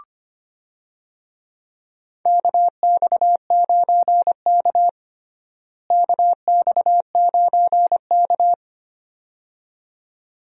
Morse code, heard as a keyed tone, about 25 wpm, 700 Hz. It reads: KX9K KX9K
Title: USA Call Signs - 4 Characters - Repeated 2x - 25wpm